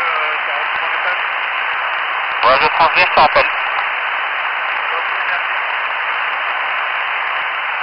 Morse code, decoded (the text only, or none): none